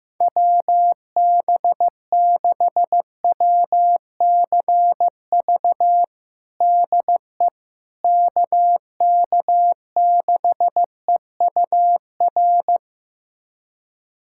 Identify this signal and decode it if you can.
Morse code, heard as a keyed tone, about 15 wpm, 700 Hz. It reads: WB6WCV DE KK6EUR
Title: USA Callsign DE Callsign - 6 Characters - 15wpm